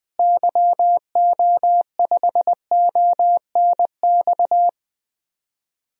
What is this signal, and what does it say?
Morse code, heard as a keyed tone, about 20 wpm, 700 Hz. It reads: YO5ONX
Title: DXCC-100 Call Signs - 6 Characters - Repeated 2x - 20wpm